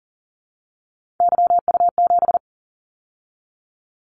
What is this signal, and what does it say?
Morse code, heard as a keyed tone, about 40 wpm, 700 Hz. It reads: YU7
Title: Callsign Prefixes for Top 25 DXCC - 40wpm